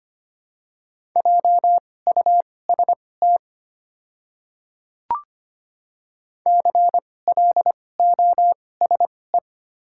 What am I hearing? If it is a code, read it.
Morse code, heard as a keyed tone, about 25 wpm, spaced as for 18 wpm, 700 Hz. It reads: JUST CLOSE
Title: 2x Character Spacing - Top 500 Words - 25wpm